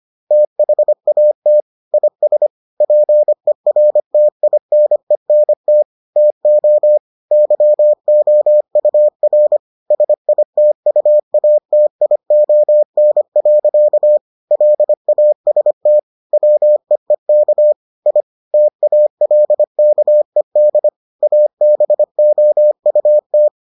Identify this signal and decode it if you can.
Morse code, heard as a keyed tone, about 25 wpm, 600 Hz. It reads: THAT IS PERTINENT TO YOUR SITUATION. LAST WEEK I TALKED ABOUT